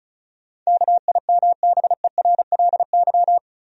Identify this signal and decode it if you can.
Morse code, heard as a keyed tone, about 36 wpm, 700 Hz. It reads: KIMBERLY
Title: USA Common Names - 35wpm